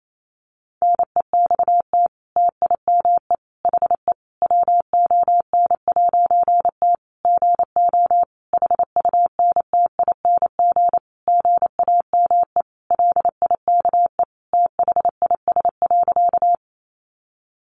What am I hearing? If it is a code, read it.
Morse code, heard as a keyed tone, about 28 wpm, 700 Hz. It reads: NEXT TIME HE WON'T GO HUNTING GAME LIKE THIS.
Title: A Sound Of Thunder 28wpm